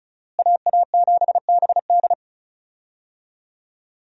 Morse code, heard as a keyed tone, about 36 wpm, 700 Hz. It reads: AA7BD
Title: USA Call Signs - 5 Characters - Repeated 2x - 35wpm